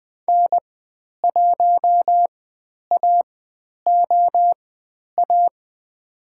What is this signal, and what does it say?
Morse code, heard as a keyed tone, about 20 wpm, spaced as for 10 wpm, 700 Hz. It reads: N1AOA
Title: 102 Lesson 17g - Letter u - Call Signs 20wpm